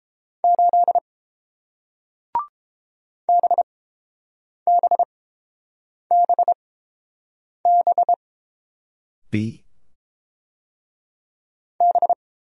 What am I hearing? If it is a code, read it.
Morse code, heard as a keyed tone, about 30 wpm, spaced as for 7 wpm, 700 Hz. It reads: 8 BBBB B